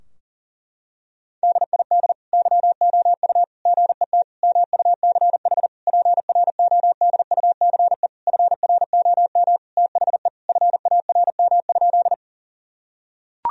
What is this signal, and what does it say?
Morse code, heard as a keyed tone, about 37 wpm, 700 Hz. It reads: DID YOU GET MUCH PRODUCE FROM THE FARM?
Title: Sentences from Top 1000 Words - Review Part 3 - 40wpm